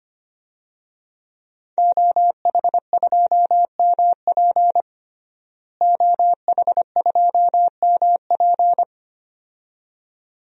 Morse code, heard as a keyed tone, about 25 wpm, 700 Hz. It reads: OH2MP OH2MP
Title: DXCC-100 Call Signs - 5 Characters - Repeated 2x - 25wpm